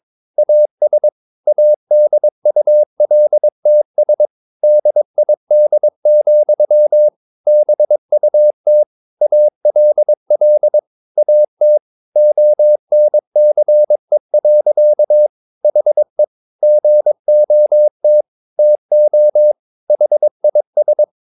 Morse code, heard as a keyed tone, about 22 wpm, 600 Hz. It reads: AS ADULTS DID, BUT ALL AT ONCE. HE GOT TO HIS
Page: Romanen Never af Ken Follet på engelsk.